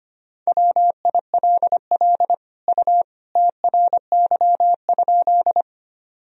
Morse code, heard as a keyed tone, about 25 wpm, 700 Hz. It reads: WILL U TRY?
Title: Sentences from Top 200 Words and Abbreviations 25wpm